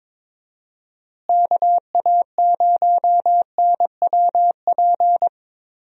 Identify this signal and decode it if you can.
Morse code, heard as a keyed tone, about 22 wpm, 700 Hz. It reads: KA0NWP
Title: USA Call Signs - 6 Characters - Repeated 2x - 22wpm